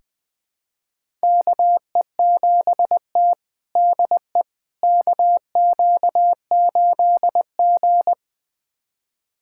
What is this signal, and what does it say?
Morse code, heard as a keyed tone, about 21 wpm, 700 Hz. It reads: KE7T DE KQ8G
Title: USA Callsign DE Callsign - 4 Characters - 20wpm